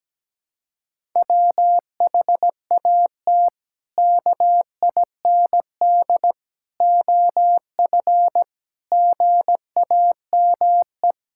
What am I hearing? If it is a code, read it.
Morse code, heard as a keyed tone, about 17 wpm, 700 Hz. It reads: WHAT KIND OF GAME
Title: Sentences from Top 1000 Words - Review Part 6 - 17wpm